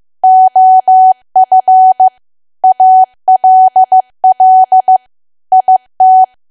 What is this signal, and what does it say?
Morse code, heard as a keyed tone, about 15 wpm, 750 Hz. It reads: OF ALL IT
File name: cwa / W1AW-Level2 / 15WPM